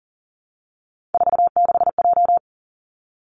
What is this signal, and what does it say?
Morse code, heard as a keyed tone, about 38 wpm, 700 Hz. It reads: 46J
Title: Three Challenging Letter-Number Combinations 40wpm